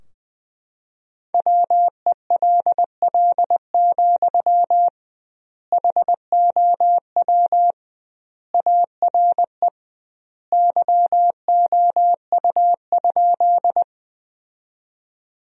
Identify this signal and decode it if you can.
Morse code, heard as a keyed tone, about 19 wpm, 700 Hz. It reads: WELL, HOW ARE YOU?
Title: 2x Word-Spacing - Sentences from Top 200 Words - 20wpm